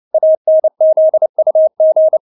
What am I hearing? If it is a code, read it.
Morse code, heard as a keyed tone, about 29 wpm, 630 Hz. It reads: ANZUG
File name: Witz10 29wpm